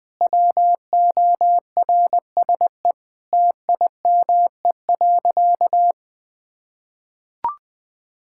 Morse code, heard as a keyed tone, about 21 wpm, 700 Hz. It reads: WORSE TIME.
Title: Sentences with 8-Word Phrases - Spoken Beginning - 20wpm